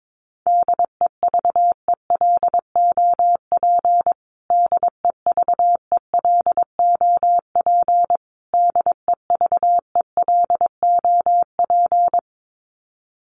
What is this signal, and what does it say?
Morse code, heard as a keyed tone, about 22 wpm, 700 Hz. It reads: DEVELOP DEVELOP DEVELOP
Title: Top 500 Words Repeated 22wpm